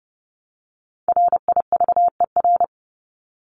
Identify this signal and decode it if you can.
Morse code, heard as a keyed tone, about 30 wpm, 700 Hz. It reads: RIVER
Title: Top 500 Words 30wpm